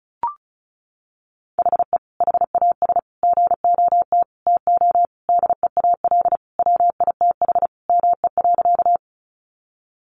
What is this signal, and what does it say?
Morse code, heard as a keyed tone, about 34 wpm, 700 Hz. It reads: HE HAS GOT TO DEAL WITH ME.
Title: Sentences from Top 800 Words 35wpm